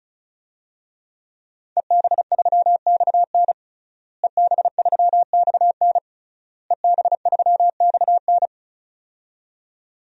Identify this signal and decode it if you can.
Morse code, heard as a keyed tone, about 34 wpm, 700 Hz. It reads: EB3XN EB3XN EB3XN
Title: DXCC-100 Call Signs - 5 Characters - Repeated 3x - 35wpm